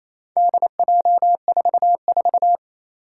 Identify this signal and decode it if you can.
Morse code, heard as a keyed tone, about 27 wpm, 700 Hz. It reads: DJ44
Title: Four Challenging Letter-Number Combinations 28wpm